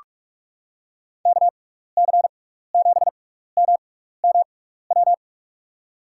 Morse code, heard as a keyed tone, about 42 wpm, spaced as for 15 wpm, 700 Hz. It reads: KC7MMW
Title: USA Call Signs - 6 Characters - 6x Character Spacing - 45wpm